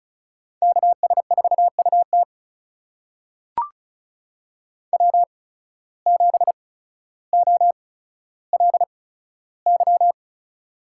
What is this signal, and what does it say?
Morse code, heard as a keyed tone, about 36 wpm, spaced as for 15 wpm, 700 Hz. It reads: KS4UT W7OLY
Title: USA Call Signs - 8x Character Spacing - 35wpm